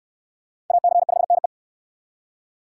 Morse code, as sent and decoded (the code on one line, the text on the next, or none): .- -.-. ..... -. .
AC5NE